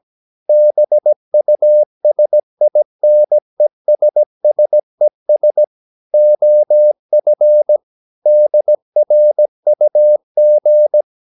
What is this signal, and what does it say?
Morse code, heard as a keyed tone, about 17 wpm, 600 Hz. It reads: BUSINESSES OF DRUG
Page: Romanen Never af Ken Follet på engelsk.